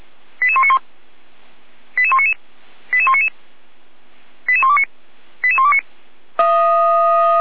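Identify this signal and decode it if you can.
Five-tone selective call (ZVEI), heard as quick runs of five tones, tones of about 70 ms. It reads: 88191 88199 88199 88129 88129
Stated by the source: ZVEI five-tone selective call 88122 received on an SDR.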